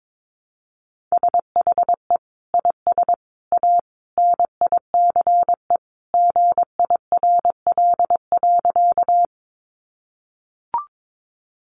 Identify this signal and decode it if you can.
Morse code, heard as a keyed tone, about 22 wpm, 700 Hz. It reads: SHE IS A NICE GIRL.
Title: Sentences from Top 1100 Words 22wpm